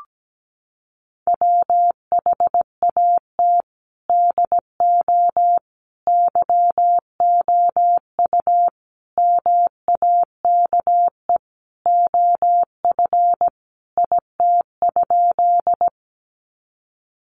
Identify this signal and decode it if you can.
Morse code, heard as a keyed tone, about 17 wpm, 700 Hz. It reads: WHAT DO YOU MAKE OF IT?
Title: Sentences from Top 100 Words 17wpm